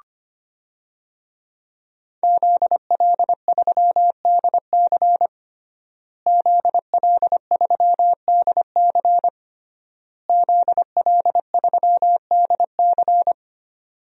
Morse code, heard as a keyed tone, about 25 wpm, 700 Hz. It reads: ZL3DC ZL3DC ZL3DC
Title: DXCC-100 Call Signs - 5 Characters - Repeated 3x - 25wpm